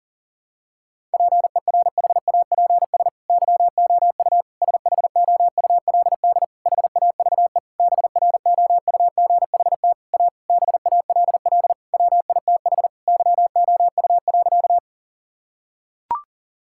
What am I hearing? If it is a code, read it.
Morse code, heard as a keyed tone, about 37 wpm, 700 Hz. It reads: PERHAPS YOU SHOULD HAVE BROUGHT A BALL WITH YOU.
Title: Sentences from Top 1000 Words - Review Part 6 - 40wpm